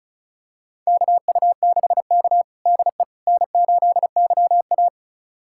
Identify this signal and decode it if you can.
Morse code, heard as a keyed tone, about 36 wpm, 700 Hz. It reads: KU6K DE N8YA
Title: USA Callsign DE Callsign - 4 Characters - 35wpm